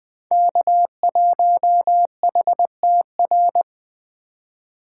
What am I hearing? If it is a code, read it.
Morse code, heard as a keyed tone, about 20 wpm, 700 Hz. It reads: K1HTR
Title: Speed Racing - USA Call Signs 17wpm